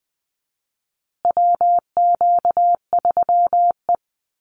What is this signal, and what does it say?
Morse code, heard as a keyed tone, about 20 wpm, 700 Hz. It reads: WQ3E
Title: Call Signs Repeated 3x - 20wpm